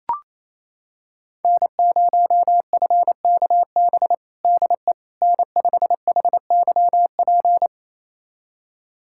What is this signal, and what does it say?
Morse code, heard as a keyed tone, about 28 wpm, 700 Hz. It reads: N0FKB DE N5HYP
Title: USA Callsign DE Callsign - 5 Characters - 28wpm